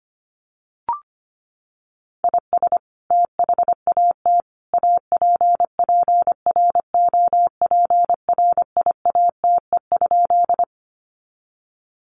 Morse code, heard as a keyed tone, about 25 wpm, 700 Hz. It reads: IS THAT APPROPRIATE?
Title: Sentences from Top 900 Words 25wpm